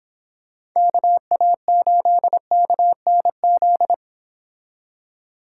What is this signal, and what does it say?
Morse code, heard as a keyed tone, about 26 wpm, 700 Hz. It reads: KA8KNZ